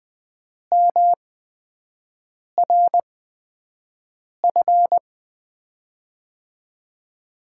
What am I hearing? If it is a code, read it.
Morse code, heard as a keyed tone, about 20 wpm, spaced as for 5 wpm, 700 Hz. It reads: MRF
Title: USA Call Signs - 6 Characters - 8x Character Spacing - 20wpm